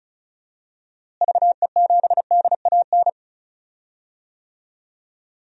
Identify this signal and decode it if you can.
Morse code, heard as a keyed tone, about 36 wpm, 700 Hz. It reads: VE7DAN
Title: DXCC-100 Call Signs - 6 Characters - Repeated 2x - 35wpm